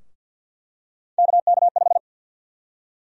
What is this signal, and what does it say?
Morse code, heard as a keyed tone, about 48 wpm, 700 Hz. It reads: KK5
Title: USA Call Sign Prefixes - Repeated 2x - 50wpm